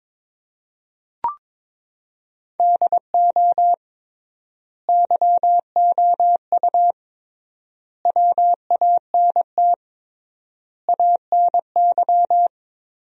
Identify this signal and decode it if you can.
Morse code, heard as a keyed tone, about 22 wpm, 700 Hz. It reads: DO YOU WANT ANY
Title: 3x Word-Spacing - Sentences from Top 400 Words - 22wpm